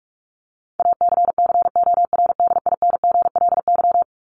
Morse code, heard as a keyed tone, about 44 wpm, 700 Hz. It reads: ACCORDINGLY
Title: LY Suffix - Top 100 Words with Base Words 45wpm